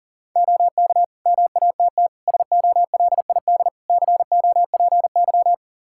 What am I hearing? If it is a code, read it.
Morse code, heard as a keyed tone, about 40 wpm, 700 Hz. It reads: OK MATT SOLID COPY
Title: Practice QSO - Exchange 3 - More Info - 40wpm